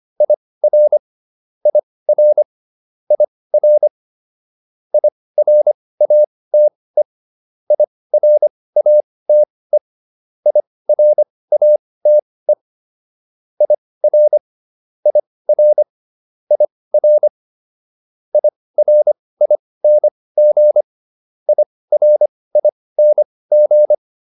Morse code, heard as a keyed tone, about 25 wpm, spaced as for 18 wpm, 600 Hz. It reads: IR IR IR IRATE IRATE IRATE IR IR IR IRING IRING